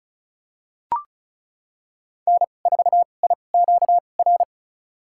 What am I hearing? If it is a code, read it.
Morse code, heard as a keyed tone, about 36 wpm, spaced as for 25 wpm, 700 Hz. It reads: N4IQR
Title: USA Call Signs - 5 Characters - 2x Character Spacing - 35wpm